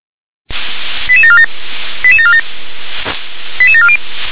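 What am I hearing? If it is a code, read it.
Five-tone selective call (ZVEI), heard as quick runs of five tones, tones of about 70 ms. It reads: 88747 88747 88740
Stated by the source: ZVEI five-tone call 88853 received by SDR.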